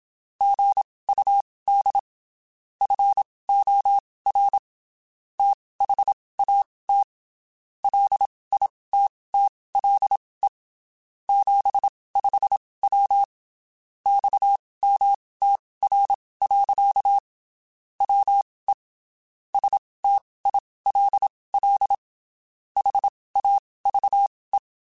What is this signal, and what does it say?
Morse code, heard as a keyed tone, about 27 wpm, spaced as for 19 wpm, 800 Hz. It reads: GUD FOR THAT LITTLE 75W XMTR. WE STILL HAVE
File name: (spaced) / QSOs Spaced-mp3s / WA7QOX 1 @25 sp